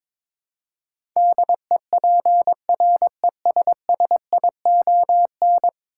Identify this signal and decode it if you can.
Morse code, heard as a keyed tone, about 22 wpm, 700 Hz. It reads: DEPRESSION
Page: morse-code-ninja-practice-22wpm